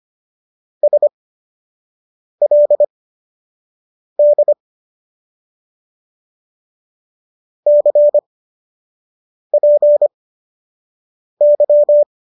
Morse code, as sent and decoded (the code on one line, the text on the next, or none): ... .-.. -.. / -.-. .--. -.--
SLD CPY